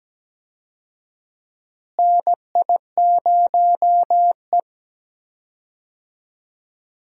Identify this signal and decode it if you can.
Morse code, heard as a keyed tone, about 17 wpm, 700 Hz. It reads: NI0E
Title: USA Call Signs - 4 Characters - Repeated 2x - 17wpm